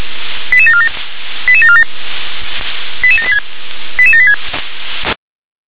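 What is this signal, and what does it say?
Five-tone selective call (ZVEI), heard as quick runs of five tones, tones of about 70 ms. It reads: 88747 88747 88786 88786
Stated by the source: ZVEI five-tone call 88740 received with an SDR.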